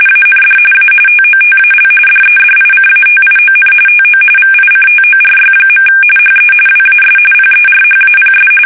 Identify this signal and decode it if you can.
RTTY, about 45.45 baud. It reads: AND A TOTAL DEDICATION TO VICTORY. HE WAS PERHAPS
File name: fsk850